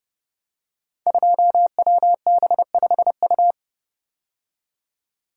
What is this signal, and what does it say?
Morse code, heard as a keyed tone, about 30 wpm, 700 Hz. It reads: 2WB5U
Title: Code Groups - 5 Letter:Numbers - 30wpm